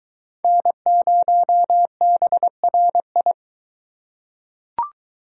Morse code, heard as a keyed tone, about 22 wpm, 700 Hz. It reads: N0BRI